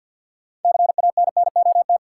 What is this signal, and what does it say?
Morse code, heard as a keyed tone, about 53 wpm, 700 Hz. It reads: CANNOT
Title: Sentences from Top 1000 Words - Review Part 6 - New Word Drill - 50wpm